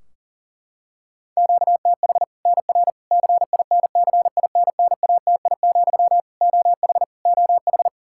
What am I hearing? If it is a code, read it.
Morse code, heard as a keyed tone, about 40 wpm, 700 Hz. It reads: QTH NR CINCINNATI, OH OH
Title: Practice QSO - Exchange 1 - Beginning - 40wpm